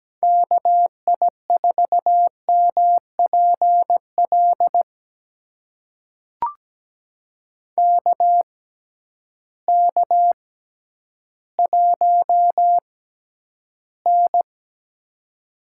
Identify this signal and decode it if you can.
Morse code, heard as a keyed tone, about 17 wpm, spaced as for 10 wpm, 700 Hz. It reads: KI4MPL KK1N
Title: USA Call Signs - 6x Character Spacing - 17wpm